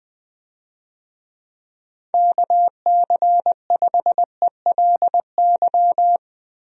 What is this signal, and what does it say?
Morse code, heard as a keyed tone, about 20 wpm, 700 Hz. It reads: KC5ELY
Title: USA Call Signs - 6 Characters - Repeated 2x - 20wpm